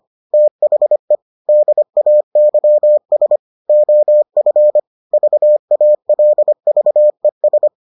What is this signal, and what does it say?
Morse code, heard as a keyed tone, about 25 wpm, 600 Hz. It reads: THE DAYS OF VALVES